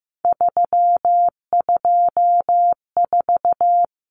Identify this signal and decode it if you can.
Morse code, heard as a keyed tone, about 15 wpm, 700 Hz. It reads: 324
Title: Numbers - 3-Digit 15wpm